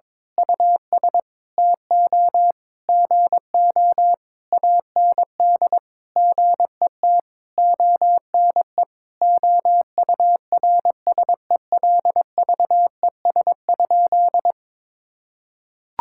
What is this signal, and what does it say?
Morse code, heard as a keyed tone, about 22 wpm, 700 Hz. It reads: US TO GO AND GET ONE OURSELVES?
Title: Sentences from Top 1600 Words 22wpm